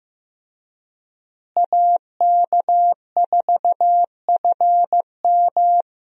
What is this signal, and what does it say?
Morse code, heard as a keyed tone, about 15 wpm, 700 Hz. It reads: AK4FM
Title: USA Call Signs - 5 Characters - Repeated 2x - 15wpm